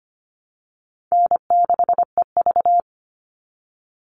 Morse code, heard as a keyed tone, about 25 wpm, 700 Hz. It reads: N6EV
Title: Call Signs Repeated 2x - 25wpm